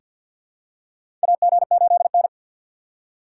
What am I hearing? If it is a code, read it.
Morse code, heard as a keyed tone, about 46 wpm, 700 Hz. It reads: AG8N
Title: USA Call Signs - 4 Characters - 3x Character Spacing - 50wpm